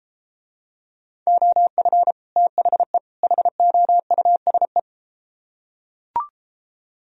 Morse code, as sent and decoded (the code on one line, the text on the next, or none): --- ..-. / - .... / .... --- ..- ... .
OF THE HOUSE